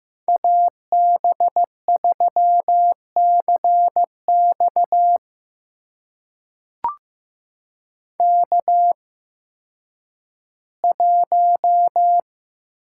Morse code, heard as a keyed tone, about 15 wpm, spaced as for 10 wpm, 700 Hz. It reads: AB3CX K1
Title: USA Call Signs - 5 Characters - 8x Character Spacing - 15wpm